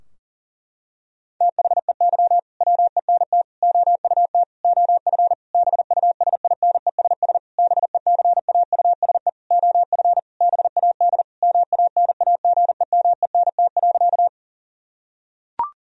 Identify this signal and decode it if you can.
Morse code, heard as a keyed tone, about 40 wpm, 700 Hz. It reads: THEY WENT OUT OF BUSINESS BECAUSE OF BAD MANAGEMENT.
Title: Sentences from Top 1000 Words - Review Part 4 - 40wpm